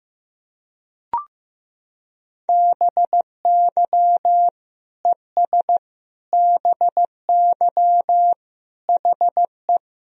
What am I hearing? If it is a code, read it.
Morse code, heard as a keyed tone, about 15 wpm, 700 Hz. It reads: BY ES BY HE
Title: Sentences from Top 300 Words and Abbreviations 15wpm